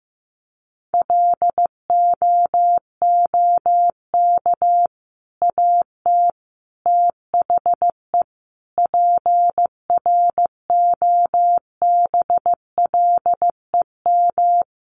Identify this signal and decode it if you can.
Morse code, heard as a keyed tone, about 14 wpm, 700 Hz. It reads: LOOK AT THE PROBLEM
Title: Sentences from Top 1500 Words 15wpm